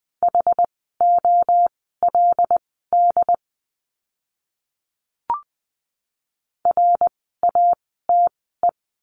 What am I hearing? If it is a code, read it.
Morse code, heard as a keyed tone, about 20 wpm, spaced as for 15 wpm, 700 Hz. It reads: HOLD RATE
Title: 101 Lesson 17f - Letter u - Words 20wpm